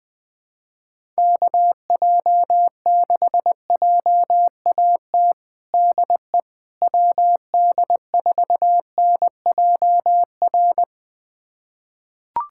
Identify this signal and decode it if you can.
Morse code, heard as a keyed tone, about 20 wpm, 700 Hz. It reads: KJ6JAT DE WD4NJR
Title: USA Callsign DE Callsign - 6 Characters - 20wpm